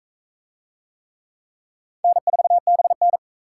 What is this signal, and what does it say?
Morse code, heard as a keyed tone, about 37 wpm, 700 Hz. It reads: N4BN